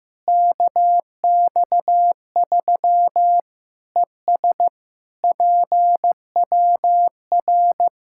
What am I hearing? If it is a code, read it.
Morse code, heard as a keyed tone, about 15 wpm, 700 Hz. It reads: KX3 ES PWR
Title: Practice QSO - Exchange 2 - Info - 15wpm